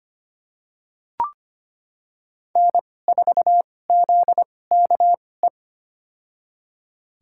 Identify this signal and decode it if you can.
Morse code, heard as a keyed tone, about 25 wpm, spaced as for 18 wpm, 700 Hz. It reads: N4ZKE
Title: USA Call Signs - 5 Characters - 2x Character Spacing - 25wpm